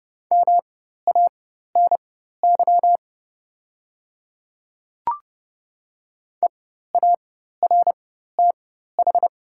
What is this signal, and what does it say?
Morse code, heard as a keyed tone, about 30 wpm, spaced as for 14 wpm, 700 Hz. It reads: MANY EARTH
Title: 4x Character Spacing - Top 200 Words - 30wpm